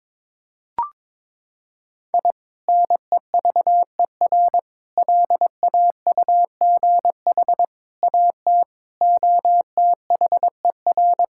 Morse code, heard as a keyed tone, about 22 wpm, 700 Hz. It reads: I NEVER LAUGH AT OTHER
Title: Sentences from Top 1000 Words - Review Part 6 - 22wpm